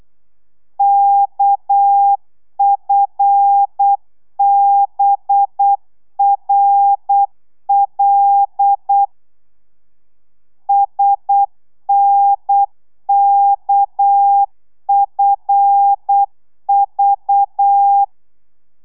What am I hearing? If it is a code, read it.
Morse code, heard as a keyed tone, about 8 wpm, 800 Hz. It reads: KFBRL SNKFV